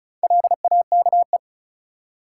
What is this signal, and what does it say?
Morse code, heard as a keyed tone, about 34 wpm, 700 Hz. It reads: LAKE
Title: Sentences from Top 1100 Words - New Word Drill Alt - 35wpm